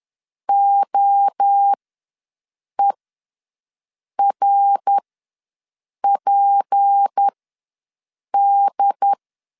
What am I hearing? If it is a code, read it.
Morse code, heard as a keyed tone, about 10 wpm, spaced as for 6 wpm, 800 Hz. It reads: OERPD
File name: LEZIONI / 7 GF9Z / 7 G ws 10wpm 3843